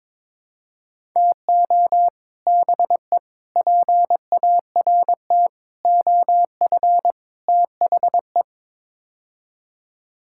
Morse code, heard as a keyed tone, about 22 wpm, 700 Hz. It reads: TO BE PART OF THE
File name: practice / 22wpm / Build A Phrase - 5-Word Phrases - 22wpm